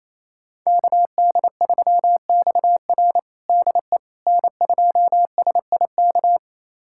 Morse code, heard as a keyed tone, about 28 wpm, 700 Hz. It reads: KD3XR DE N2SIK
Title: USA Callsign DE Callsign - 5 Characters - 28wpm